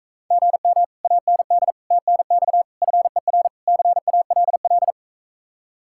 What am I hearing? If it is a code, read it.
Morse code, heard as a keyed tone, about 44 wpm, 700 Hz. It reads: GM AND TNX FER CALL